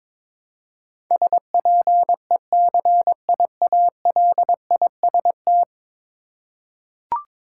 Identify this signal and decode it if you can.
Morse code, heard as a keyed tone, about 23 wpm, 700 Hz. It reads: SPECIALIST
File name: practice / 22wpm / 10 Letter Words - First 3 Letters Given - 22wpm